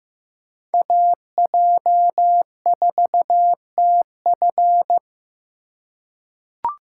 Morse code, heard as a keyed tone, about 15 wpm, 700 Hz. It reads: AJ4TF
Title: K1USN SST QSO Practice - 15wpm